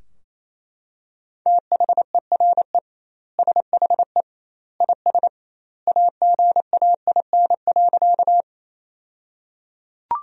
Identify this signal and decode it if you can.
Morse code, heard as a keyed tone, about 27 wpm, 700 Hz. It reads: THERE SHE IS AGAIN.
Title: 2x Word-Spacing - Sentences from Top 200 Words - 28wpm